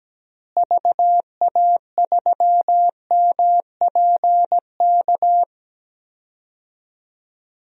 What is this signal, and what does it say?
Morse code, heard as a keyed tone, about 17 wpm, 700 Hz. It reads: VA3MPK
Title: DXCC-100 Call Signs - 6 Characters - Repeated 2x - 17wpm